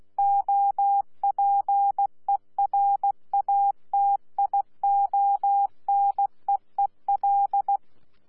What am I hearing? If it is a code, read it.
Morse code, heard as a keyed tone, about 16 wpm, 800 Hz. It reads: OPERATIONEEL
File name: Z PI4VRZ / PI4VRZ 14-jun-2025